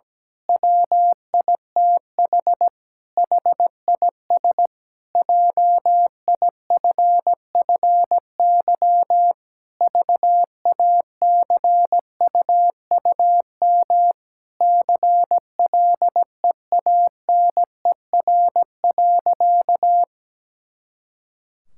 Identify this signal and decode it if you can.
Morse code, heard as a keyed tone, about 17 wpm, 700 Hz. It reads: WITH HIS JIFFY VACUUM CLEANER.
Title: Pangram Challenge v2 - 17wpm